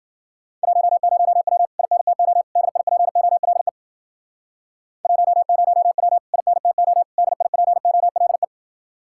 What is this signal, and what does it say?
Morse code, heard as a keyed tone, about 50 wpm, 700 Hz. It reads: 10W INTO DIPOLE 10W INTO DIPOLE
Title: Speed-Racing Common QSO Phrases Encore - 40wpm